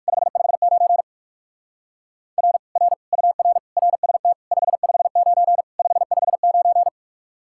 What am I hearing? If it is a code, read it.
Morse code, heard as a keyed tone, about 59 wpm, 700 Hz. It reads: 559 R R UR RST 559 559